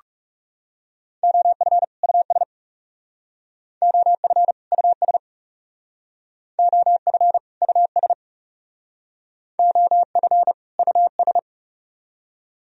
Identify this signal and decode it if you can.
Morse code, heard as a keyed tone, about 38 wpm, 700 Hz. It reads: OF US OF US OF US OF US
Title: Speed-Racing - Sets of 2 Words 30wpm